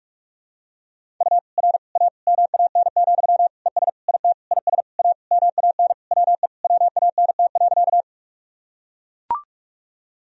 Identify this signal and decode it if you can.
Morse code, heard as a keyed tone, about 42 wpm, 700 Hz. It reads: U R A MAN, ES IT IS A MAN WE WANT.
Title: Sentences from Top 200 Words and Abbreviations 45wpm